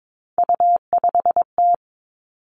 Morse code, heard as a keyed tone, about 22 wpm, 700 Hz. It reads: U5T
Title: Three Challenging Letter-Number Combinations 22wpm